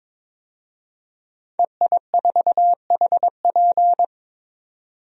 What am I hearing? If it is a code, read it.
Morse code, heard as a keyed tone, about 23 wpm, 700 Hz. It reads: EI4HP
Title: DXCC-100 Call Signs - 5 Characters - 22wpm